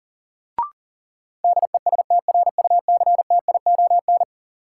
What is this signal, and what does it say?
Morse code, heard as a keyed tone, about 40 wpm, 700 Hz. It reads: DESTRUCTION
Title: Sets of 5 Words Encore - New Word Drill - 40wpm